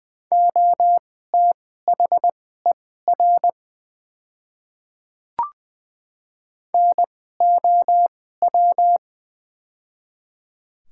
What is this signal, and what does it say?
Morse code, heard as a keyed tone, about 20 wpm, spaced as for 14 wpm, 700 Hz. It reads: OTHER NOW
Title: 2x Character Spacing - Top 100 Words - 20wpm